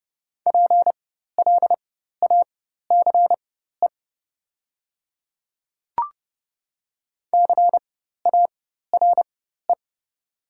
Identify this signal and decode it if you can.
Morse code, heard as a keyed tone, about 29 wpm, spaced as for 14 wpm, 690 Hz. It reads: PLACE CARE
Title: 4x Character Spacing - Top 500 Words - 30wpm